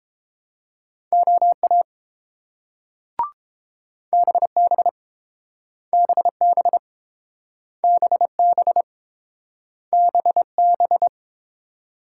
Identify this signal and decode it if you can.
Morse code, heard as a keyed tone, about 30 wpm, 700 Hz. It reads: OA BB BB BB BB